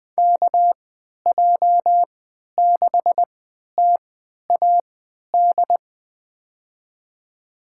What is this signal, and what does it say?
Morse code, heard as a keyed tone, about 20 wpm, spaced as for 11 wpm, 700 Hz. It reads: KJ6TAD
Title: USA Call Signs - 6 Characters - 3x Character Spacing - 20wpm